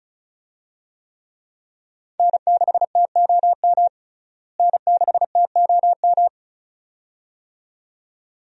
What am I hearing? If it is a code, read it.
Morse code, heard as a keyed tone, about 34 wpm, 700 Hz. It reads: N6TOM N6TOM
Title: USA Call Signs - 5 Characters - Repeated 2x - 35wpm